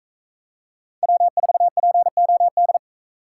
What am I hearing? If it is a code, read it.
Morse code, heard as a keyed tone, about 42 wpm, 700 Hz. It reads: W4POD